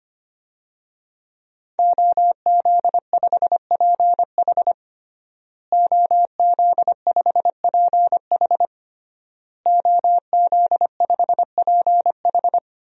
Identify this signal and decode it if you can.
Morse code, heard as a keyed tone, about 25 wpm, 700 Hz. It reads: OZ5PH OZ5PH OZ5PH
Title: DXCC-100 Call Signs - 5 Characters - Repeated 3x - 25wpm